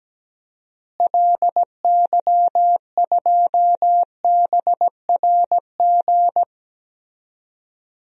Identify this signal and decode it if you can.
Morse code, heard as a keyed tone, about 17 wpm, 700 Hz. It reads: LY2BRG
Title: DXCC-100 Call Signs - 6 Characters - Repeated 3x - 17wpm